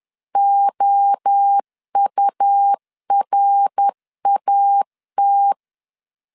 Morse code, heard as a keyed tone, about 10 wpm, 800 Hz. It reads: OURAT
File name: CW / DET / 1 3 1500C S C 10wpm 3335